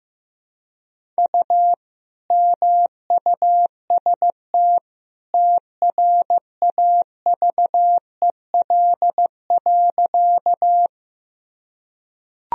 Morse code, heard as a keyed tone, about 15 wpm, 700 Hz. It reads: U MUST TRAVEL.
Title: Sentences from Top 400 Words and Abbreviations 15wpm